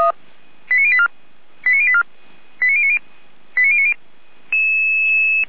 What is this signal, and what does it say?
Five-tone selective call (ZVEI), heard as quick runs of five tones, tones of about 70 ms. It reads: 89084 89084 89090 89090